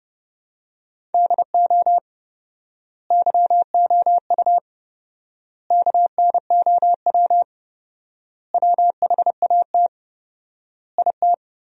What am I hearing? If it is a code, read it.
Morse code, heard as a keyed tone, about 30 wpm, 700 Hz. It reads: DO YOU KNOW WHAT IT
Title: 4x Word-Spacing - Sentences from Top 100 Words - 30wpm